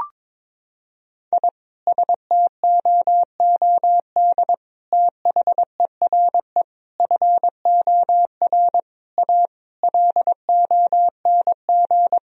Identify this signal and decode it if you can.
Morse code, heard as a keyed tone, about 21 wpm, 700 Hz. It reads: I STOOD THERE FOR A LONG
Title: Sentences from Top 1000 Words - Review Part 2 - 22wpm